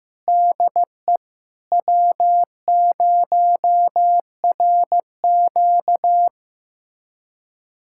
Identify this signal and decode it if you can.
Morse code, heard as a keyed tone, about 15 wpm, 700 Hz. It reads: DE W0RQ
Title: USA Callsign DE Callsign - 4 Characters - 15wpm